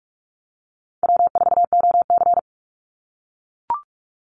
Morse code, heard as a keyed tone, about 44 wpm, 700 Hz. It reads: W4OC
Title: Easy Call Signs - 45wpm